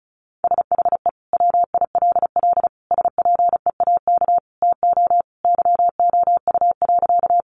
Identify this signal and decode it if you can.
Morse code, heard as a keyed tone, about 34 wpm, 700 Hz. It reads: SHE WILL SPEAK TO YOU.
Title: Sentences from Top 1100 Words 35wpm